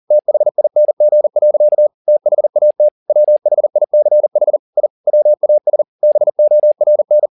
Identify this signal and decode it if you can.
Morse code, heard as a keyed tone, about 40 wpm, 600 Hz. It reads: THING. THAT WHICH I WAS BORN